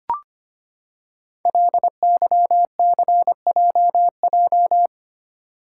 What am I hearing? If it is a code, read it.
Morse code, heard as a keyed tone, about 25 wpm, 700 Hz. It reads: LYCJJ
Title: Code Groups - 5 Letter:Numbers - 25wpm